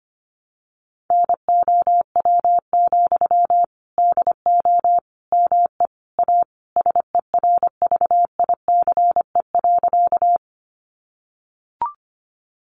Morse code, heard as a keyed tone, about 25 wpm, 700 Hz. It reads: NOW, DO ME A SERVICE.
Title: Sentences from Top 700 Words 25wpm